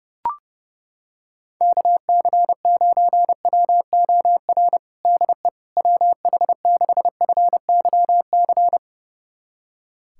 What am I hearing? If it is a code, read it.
Morse code, heard as a keyed tone, about 32 wpm, 700 Hz. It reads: KC9WOR DE WH6FYC
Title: USA Callsign DE Callsign 30wpm